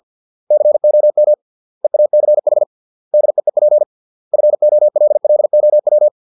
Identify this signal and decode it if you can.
Morse code, heard as a keyed tone, about 50 wpm, 600 Hz. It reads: COM EACH DEEP FOLLOW